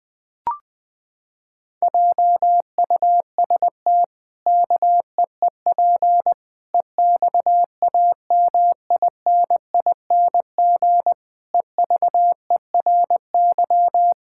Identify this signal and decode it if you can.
Morse code, heard as a keyed tone, about 20 wpm, 700 Hz. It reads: JUST KEEP EXAMINING EVERY
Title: Pangram Challenge v2 - 20wpm